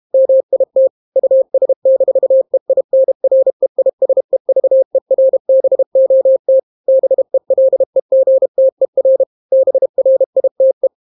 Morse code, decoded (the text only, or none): MIT US-EINREISEVERBOT BELEGTER BRITE